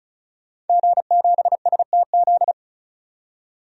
Morse code, heard as a keyed tone, about 36 wpm, 700 Hz. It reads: G7STZ
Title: DXCC-100 Call Signs - 5 Characters - 35wpm